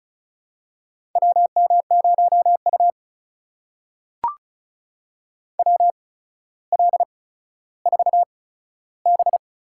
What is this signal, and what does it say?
Morse code, heard as a keyed tone, about 36 wpm, spaced as for 15 wpm, 700 Hz. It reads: WM0U WL4B
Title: USA Call Signs - 4 Characters - 8x Character Spacing - 35wpm